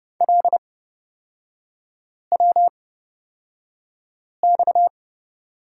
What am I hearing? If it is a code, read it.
Morse code, heard as a keyed tone, about 30 wpm, spaced as for 5 wpm, 700 Hz. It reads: LWX
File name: Lesson 37e - Letter x - Three Characters 30wpm